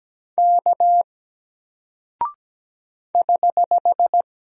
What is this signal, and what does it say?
Morse code, heard as a keyed tone, about 17 wpm, 700 Hz. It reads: K <HH>
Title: Single Character (Letter:Number:Punctuation:Prosign) 17wpm